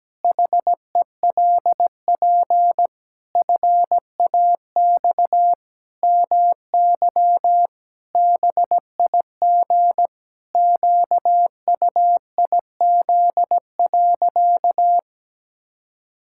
Morse code, heard as a keyed tone, about 17 wpm, 700 Hz. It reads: HELP FAX MY BIG QUIZ.
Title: Pangram Challenge v2 - 17wpm